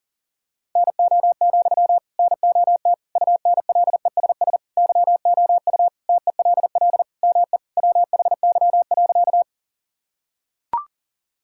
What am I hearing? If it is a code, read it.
Morse code, heard as a keyed tone, about 40 wpm, 700 Hz. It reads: NO, NOT UNLESS YOU TELL ME WHY.
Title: Sentences from Top 1000 Words - Review Part 6 - 40wpm